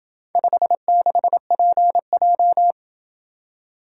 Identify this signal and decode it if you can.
Morse code, heard as a keyed tone, about 27 wpm, 700 Hz. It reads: 56PJ